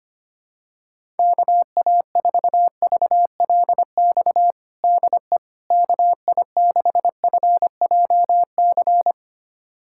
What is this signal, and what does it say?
Morse code, heard as a keyed tone, about 25 wpm, 700 Hz. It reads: KA4VLX DE KI6FJC
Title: USA Callsign DE Callsign - 6 Characters - 25wpm